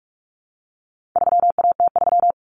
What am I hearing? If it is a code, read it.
Morse code, heard as a keyed tone, about 44 wpm, 700 Hz. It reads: 3AT3
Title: Common Cut Numbers - 4-Digit 45wpm